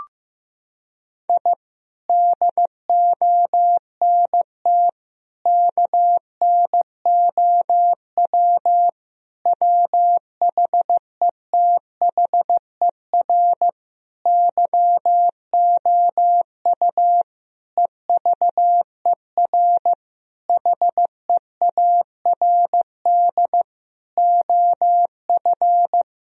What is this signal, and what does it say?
Morse code, heard as a keyed tone, about 15 wpm, 700 Hz. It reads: I DONT KNOW WHETHER YOU EVER HEARD OF
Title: Sentences from Top 1000 Words - Review Part 3 - 15wpm